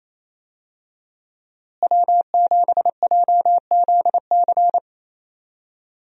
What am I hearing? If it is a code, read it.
Morse code, heard as a keyed tone, about 28 wpm, 700 Hz. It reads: W7JZC
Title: USA Call Signs - 5 Characters - Repeated 2x - 28wpm